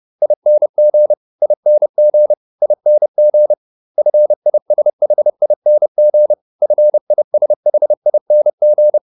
Morse code, heard as a keyed tone, about 30 wpm, 600 Hz. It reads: ING ING ING FISHING FISHING